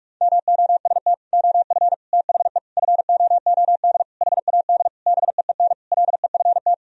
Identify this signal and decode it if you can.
Morse code, heard as a keyed tone, about 46 wpm, 700 Hz. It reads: MOST OF THE FOOD HAD BEEN LEFT
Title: Sentences from Top 1000 Words - Review Part 1 - 45wpm